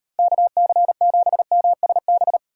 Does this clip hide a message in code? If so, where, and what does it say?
Yes: Morse code from 0.2 to 2.4 s, heard as a keyed tone, about 38 wpm, 700 Hz. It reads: KC7MSB